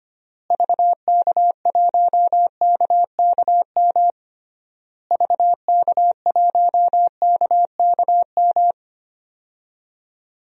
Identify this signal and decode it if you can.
Morse code, heard as a keyed tone, about 25 wpm, 700 Hz. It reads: VK1KKM VK1KKM
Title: DXCC-100 Call Signs - 6 Characters - Repeated 3x - 25wpm